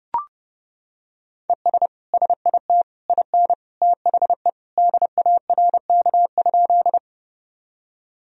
Morse code, heard as a keyed tone, about 28 wpm, 700 Hz. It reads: ES SIT IN THE DARK?
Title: Sentences from Top 500 Words and Abbreviations 30wpm